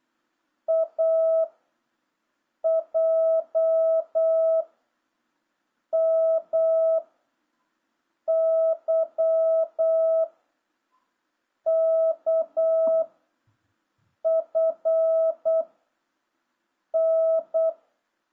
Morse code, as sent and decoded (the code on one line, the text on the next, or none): .- / .--- / -- / -.-- / -.- / ..-. / -.
A J M Y K F N